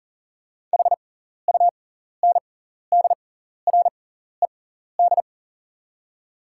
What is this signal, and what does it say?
Morse code, heard as a keyed tone, about 40 wpm, spaced as for 14 wpm, 700 Hz. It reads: HUNDRED
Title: 6x Character Spacing - Top 500 Words - 40wpm